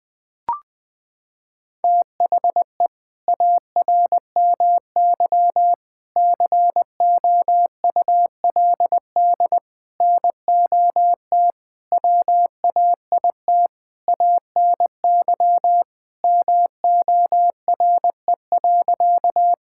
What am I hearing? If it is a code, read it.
Morse code, heard as a keyed tone, about 20 wpm, 700 Hz. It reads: THE ARMY COULD NOT WAIT ANY MORE.
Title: Sentences from Top 1000 Words - Review Part 5 - 20wpm